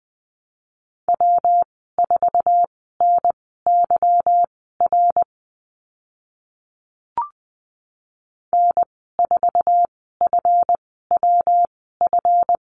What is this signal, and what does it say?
Morse code, heard as a keyed tone, about 20 wpm, spaced as for 14 wpm, 700 Hz. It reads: W4NYR N4FWF
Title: 161 Lesson 26g - Letter g - Call Signs 20wpm